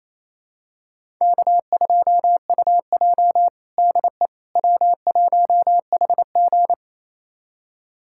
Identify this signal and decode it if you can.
Morse code, heard as a keyed tone, about 28 wpm, 700 Hz. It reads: K2UJ DE W1HG
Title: USA Callsign DE Callsign - 4 Characters - 28wpm